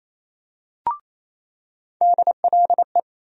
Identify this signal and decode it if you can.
Morse code, heard as a keyed tone, about 28 wpm, 700 Hz. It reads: DLE
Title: USA Call Sign Postfixes - 3 Letters 28wpm